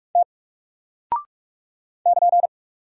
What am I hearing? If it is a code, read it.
Morse code, heard as a keyed tone, about 46 wpm, 700 Hz. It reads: T <KN>
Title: Single Character (Letter:Number:Punctuation:Prosign) 45wpm